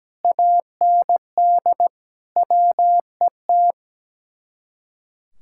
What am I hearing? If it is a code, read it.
Morse code, heard as a keyed tone, about 17 wpm, 700 Hz. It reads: AND WET
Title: Speed-Racing Common QSO Phrases Encore - 17wpm